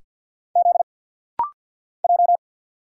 Morse code, heard as a keyed tone, about 50 wpm, 700 Hz. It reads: Z J
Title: Single Letters - Rapid-Fire 50wpm